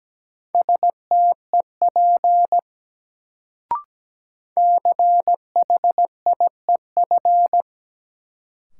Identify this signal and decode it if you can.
Morse code, heard as a keyed tone, about 17 wpm, 700 Hz. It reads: STEP CHIEF
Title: Sets of 5 Words Encore - New Word Drill - 17wpm